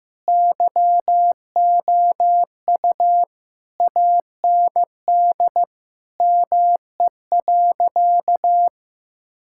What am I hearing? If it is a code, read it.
Morse code, heard as a keyed tone, about 15 wpm, 700 Hz. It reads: YOU AND ME.
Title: Sentences from Top 1600 Words 15wpm